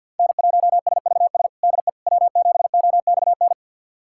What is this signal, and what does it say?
Morse code, heard as a keyed tone, about 48 wpm, 700 Hz. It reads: N1SVS DE W7OXN
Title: USA Callsign DE Callsign - 5 Characters - 50wpm